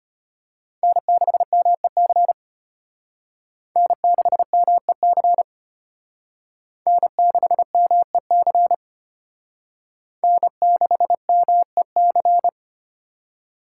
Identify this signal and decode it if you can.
Morse code, heard as a keyed tone, about 32 wpm, 700 Hz. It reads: N6MEC N6MEC N6MEC N6MEC